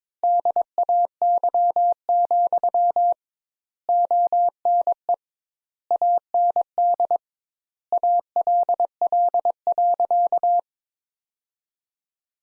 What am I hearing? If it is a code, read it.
Morse code, heard as a keyed tone, about 22 wpm, 700 Hz. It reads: DAY, ONE AND ALL.
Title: Sentences Top 200 Words - 22wpm